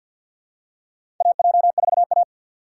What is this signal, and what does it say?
Morse code, heard as a keyed tone, about 50 wpm, 700 Hz. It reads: AJ4A
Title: USA Call Signs - 6x Character Spacing - 50wpm